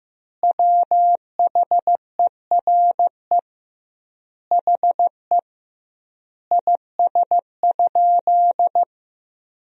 Morse code, heard as a keyed tone, about 15 wpm, 700 Hz. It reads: WHERE HE IS?
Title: 2x Word-Spacing - Sentences from Top 300 Words - 15wpm